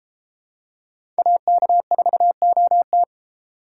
Morse code, heard as a keyed tone, about 32 wpm, 700 Hz. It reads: AK4OT